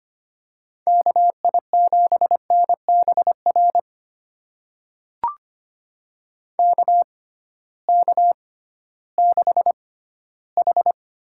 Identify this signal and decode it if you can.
Morse code, heard as a keyed tone, about 25 wpm, spaced as for 15 wpm, 700 Hz. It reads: KI7NBR KK6H
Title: USA Call Signs - 6 Characters - 6x Character Spacing - 25wpm